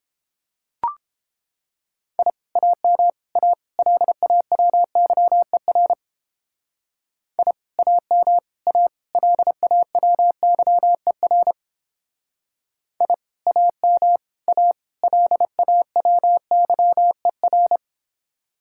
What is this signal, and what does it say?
Morse code, heard as a keyed tone, about 29 wpm, 700 Hz. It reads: I AM A LAWYER I AM A LAWYER I AM A LAWYER